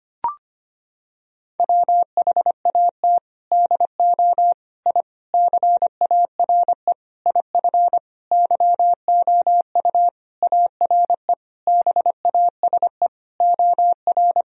Morse code, heard as a keyed tone, about 25 wpm, 700 Hz. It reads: WHAT DO I CARE IF YOU ARE BASE OR
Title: Sentences from Top 1000 Words - Review Part 1 - 25wpm